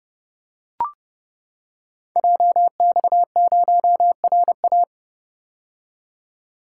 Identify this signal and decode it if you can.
Morse code, heard as a keyed tone, about 30 wpm, 700 Hz. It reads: JX0RA
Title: Code Groups - 5 Letter:Numbers - 30wpm